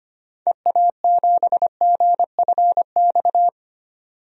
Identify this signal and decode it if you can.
Morse code, heard as a keyed tone, about 25 wpm, 700 Hz. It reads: EA7GFX
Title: DXCC-100 Call Signs - 6 Characters - 25wpm